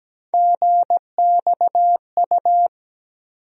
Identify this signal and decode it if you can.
Morse code, heard as a keyed tone, about 17 wpm, 700 Hz. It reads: GXU
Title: USA Call Sign Postfixes - 3 Letters 17wpm